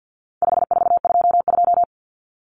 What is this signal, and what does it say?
Morse code, heard as a keyed tone, about 50 wpm, 700 Hz. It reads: 5422
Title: Numbers - 4-Digit 50wpm